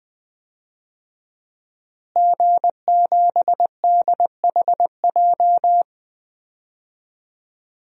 Morse code, heard as a keyed tone, about 21 wpm, 700 Hz. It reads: G7DHJ
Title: DXCC-100 Call Signs - 5 Characters - Repeated 3x - 20wpm